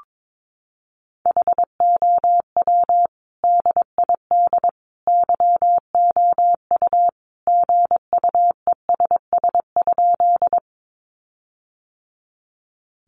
Morse code, heard as a keyed tone, about 22 wpm, 700 Hz. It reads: HOW DID YOU GUESS?
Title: Sentences from Top 1500 Words 22wpm